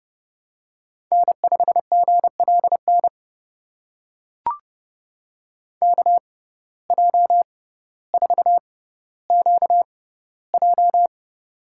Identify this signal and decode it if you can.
Morse code, heard as a keyed tone, about 31 wpm, spaced as for 15 wpm, 700 Hz. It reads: N5GLN KJ4QJ
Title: USA Call Signs - 5 Characters - 6x Character Spacing - 30wpm